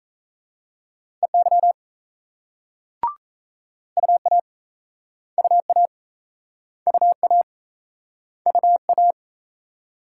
Morse code, heard as a keyed tone, about 35 wpm, 700 Hz. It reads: EY UA UA UA UA